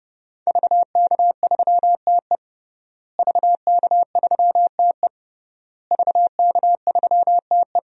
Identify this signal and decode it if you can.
Morse code, heard as a keyed tone, about 30 wpm, 700 Hz. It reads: VK3TE VK3TE VK3TE
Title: DXCC-100 Call Signs - 5 Characters - Repeated 3x - 30wpm